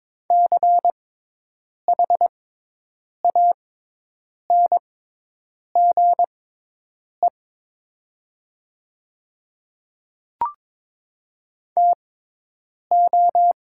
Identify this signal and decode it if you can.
Morse code, heard as a keyed tone, about 22 wpm, spaced as for 8 wpm, 700 Hz. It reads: CHANGE TO
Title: 6x Character Spacing - Top 300 Words - 22wpm